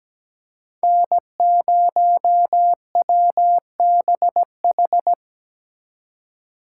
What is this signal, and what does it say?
Morse code, heard as a keyed tone, about 20 wpm, 700 Hz. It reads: N0WBH
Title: Speed Racing - USA 1x3 Call Signs 17wpm